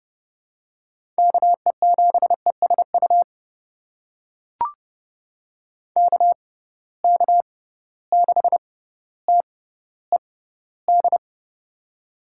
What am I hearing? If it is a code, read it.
Morse code, heard as a keyed tone, about 29 wpm, spaced as for 15 wpm, 700 Hz. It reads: KE7ESU KK6TED
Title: USA Call Signs - 6 Characters - 6x Character Spacing - 30wpm